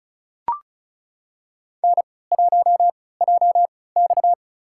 Morse code, heard as a keyed tone, about 34 wpm, spaced as for 20 wpm, 700 Hz. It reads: N1JX
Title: USA Call Signs - 4 Characters - 3x Character Spacing - 35wpm